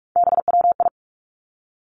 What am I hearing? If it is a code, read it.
Morse code, heard as a keyed tone, about 46 wpm, 700 Hz. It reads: BWI
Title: Three Challenging Letter-Number Combinations 45wpm